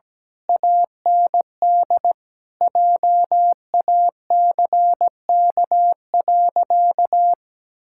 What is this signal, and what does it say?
Morse code, heard as a keyed tone, about 17 wpm, 700 Hz. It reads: AND JACK.
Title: Pangram Challenge v2 - 17wpm